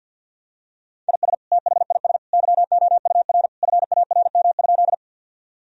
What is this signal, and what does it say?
Morse code, heard as a keyed tone, about 48 wpm, 700 Hz. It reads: IS THIS YOUR FARM?
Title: Sentences from Top 1000 Words - Review Part 2 - 50wpm